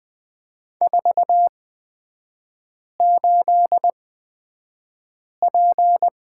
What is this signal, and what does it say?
Morse code, heard as a keyed tone, about 20 wpm, spaced as for 5 wpm, 700 Hz. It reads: 48P
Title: 243 Lesson 39e - Letter z - Three Characters 20wpm